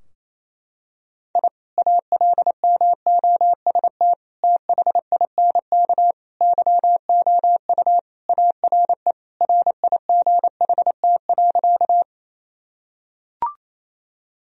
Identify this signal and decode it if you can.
Morse code, heard as a keyed tone, about 28 wpm, 700 Hz. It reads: I ALMOST THINK YOU ARE RIGHT.
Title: Sentences from Top 1000 Words - Review Part 4 - 28wpm